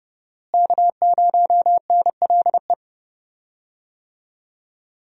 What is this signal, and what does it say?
Morse code, heard as a keyed tone, about 31 wpm, 700 Hz. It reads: K0NLE
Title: USA Call Signs - 5 Characters - Repeated 3x - 30wpm